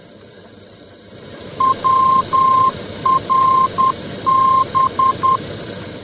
Morse code, heard as a keyed tone, about 10 wpm, 1050 Hz, with noise. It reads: WRB